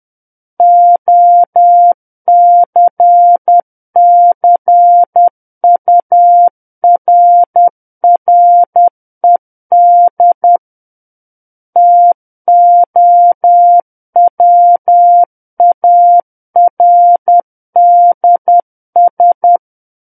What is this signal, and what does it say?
Morse code, heard as a keyed tone, about 10 wpm, 700 Hz. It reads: OCCURRED TOWARDS
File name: Book One / 01-Chapter 1 - 10 WPM